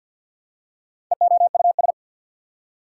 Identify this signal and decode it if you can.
Morse code, heard as a keyed tone, about 48 wpm, 700 Hz. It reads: EOUS
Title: Prefixes and Suffixes 50wpm